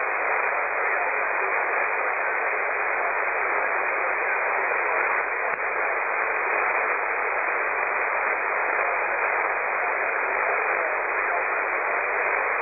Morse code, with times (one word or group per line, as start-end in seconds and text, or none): none